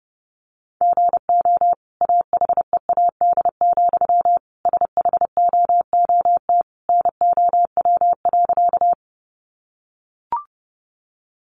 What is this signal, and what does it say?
Morse code, heard as a keyed tone, about 30 wpm, 700 Hz. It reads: GO AHEAD, SHOOT NOW.
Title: Sentences from Top 1200 Words 30wpm